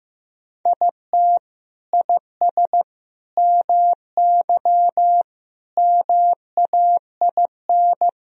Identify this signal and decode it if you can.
Morse code, heard as a keyed tone, about 15 wpm, 700 Hz. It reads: IT IS MY MAIN
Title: Sentences from Top 1000 Words - Review Part 2 - 15wpm